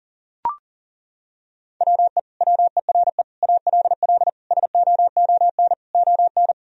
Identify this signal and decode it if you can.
Morse code, heard as a keyed tone, about 40 wpm, 700 Hz. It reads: WE WERE ALL SOON ON
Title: Sentences from Top 1000 Words - Review Part 4 - 40wpm